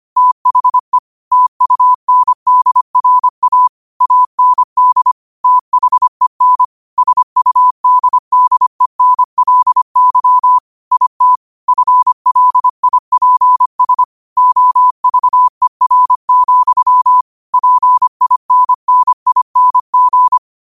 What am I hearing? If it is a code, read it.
Morse code, heard as a keyed tone, about 25 wpm, 1000 Hz. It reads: THE TUNDRA AND THEN SUDDENLY IT FLIPS OVER, PINNING
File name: 25wpm